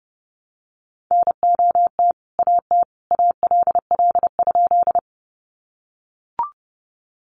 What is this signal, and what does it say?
Morse code, heard as a keyed tone, about 28 wpm, 700 Hz. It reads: NOT AT ALL?
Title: Sentences from Top 600 Words 30wpm